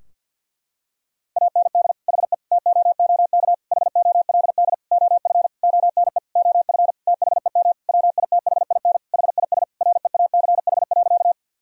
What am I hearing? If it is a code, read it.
Morse code, heard as a keyed tone, about 53 wpm, 700 Hz. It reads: AND HE TOOK HOLD OF ONE OF THEM WITHIN HIS REACH.
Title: Sentences from Top 1000 Words - Review Part 4 - 50wpm